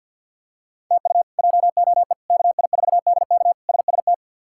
Morse code, heard as a keyed tone, about 50 wpm, 700 Hz. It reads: TU JOE KI4DK SST
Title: K1USN SST QSO Practice - 50wpm